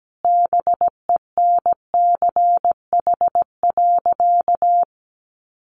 Morse code, heard as a keyed tone, about 17 wpm, 700 Hz. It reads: BENCH.
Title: Sentences from Top 1400 Words 17wpm